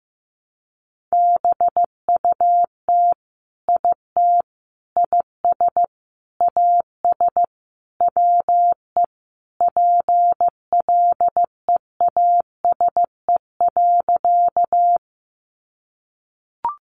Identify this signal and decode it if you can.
Morse code, heard as a keyed tone, about 15 wpm, 700 Hz. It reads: BUT IT IS AS WE PLEASE.
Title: Sentences from Top 1500 Words 15wpm